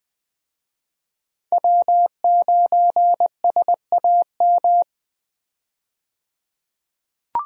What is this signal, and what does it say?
Morse code, heard as a keyed tone, about 20 wpm, 700 Hz. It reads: W9SAM
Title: USA Call Signs - 5 Characters - 20wpm